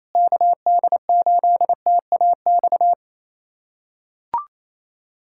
Morse code, heard as a keyed tone, about 28 wpm, 700 Hz. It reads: KD8TAX
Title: USA Callsign DE Callsign 28wpm